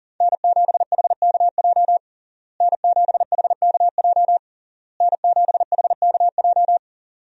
Morse code, heard as a keyed tone, about 40 wpm, 700 Hz. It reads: N7HKJ N7HKJ N7HKJ
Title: USA Call Signs - 5 Characters - Repeated 3x - 40wpm